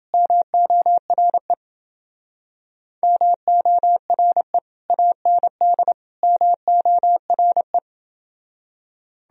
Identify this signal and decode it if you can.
Morse code, heard as a keyed tone, about 29 wpm, 700 Hz. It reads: MORE MORE AND MORE